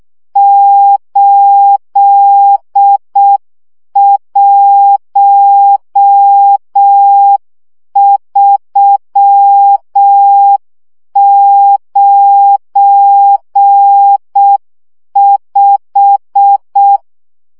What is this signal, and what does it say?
Morse code, heard as a keyed tone, about 6 wpm, 800 Hz. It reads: 81395
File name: morze vizsga 2011-2 5